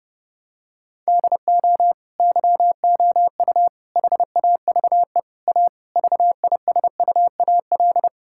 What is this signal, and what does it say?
Morse code, heard as a keyed tone, about 30 wpm, 700 Hz. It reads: DO YOU HAVE A VISUAL
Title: Sentences from Top 1000 Words - Review Part 6 - 30wpm